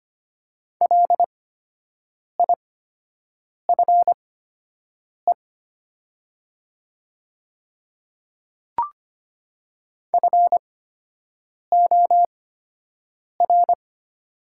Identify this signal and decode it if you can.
Morse code, heard as a keyed tone, about 25 wpm, spaced as for 7 wpm, 700 Hz. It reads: LIFE FOR
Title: 8x Character Spacing - Top 500 Words - 25wpm